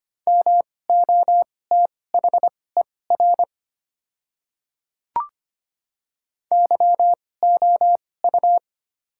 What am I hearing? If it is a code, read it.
Morse code, heard as a keyed tone, about 25 wpm, spaced as for 18 wpm, 700 Hz. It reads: MOTHER YOU
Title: 2x Character Spacing - Top 200 Words - 25wpm